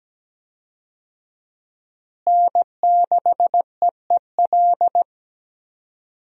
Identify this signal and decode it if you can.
Morse code, heard as a keyed tone, about 17 wpm, 700 Hz. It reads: N6EEL
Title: USA Call Signs - 5 Characters - Repeated 2x - 17wpm